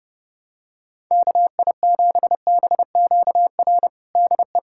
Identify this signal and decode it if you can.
Morse code, heard as a keyed tone, about 32 wpm, 700 Hz. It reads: KI7BQR DE
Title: USA Callsign DE Callsign 30wpm